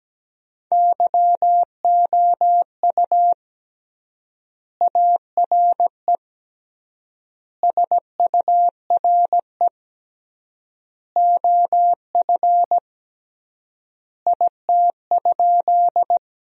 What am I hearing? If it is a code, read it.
Morse code, heard as a keyed tone, about 17 wpm, 700 Hz. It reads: YOU ARE SURE OF IT?
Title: 3x Word-Spacing - Sentences from Top 300 Words - 17wpm